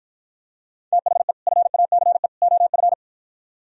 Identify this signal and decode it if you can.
Morse code, heard as a keyed tone, about 52 wpm, 700 Hz. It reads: THE FACE OF